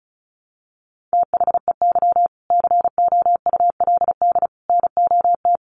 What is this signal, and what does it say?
Morse code, heard as a keyed tone, about 34 wpm, 700 Hz. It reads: THEY COULD NOT
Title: Sentences from Top 1100 Words 35wpm